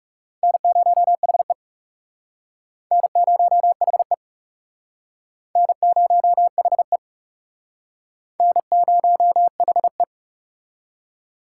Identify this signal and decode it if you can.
Morse code, heard as a keyed tone, about 38 wpm, 690 Hz. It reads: N0HE N0HE N0HE N0HE
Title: Speed Racing - USA Call Signs 30wpm